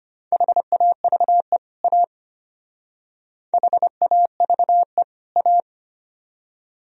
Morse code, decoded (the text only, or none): HAVE A HAVE A